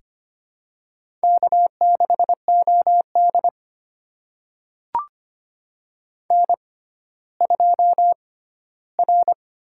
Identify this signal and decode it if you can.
Morse code, heard as a keyed tone, about 25 wpm, spaced as for 14 wpm, 700 Hz. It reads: K6OD N2R
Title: USA Call Signs - 4 Characters - 6x Character Spacing - 25wpm